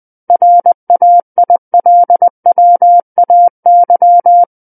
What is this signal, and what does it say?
Morse code, heard as a keyed tone, about 20 wpm, 700 Hz. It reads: RAILWAY
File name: 20 WPM